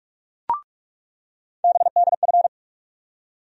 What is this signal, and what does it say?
Morse code, heard as a keyed tone, about 45 wpm, 700 Hz. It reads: BDF
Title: USA Call Sign Postfixes - 3 Letters 45wpm